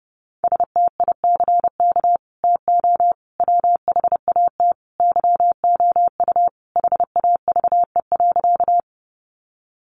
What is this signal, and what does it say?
Morse code, heard as a keyed tone, about 30 wpm, 700 Hz. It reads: STICK TO WHAT YOU HAVE.
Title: Sentences from Top 1200 Words 30wpm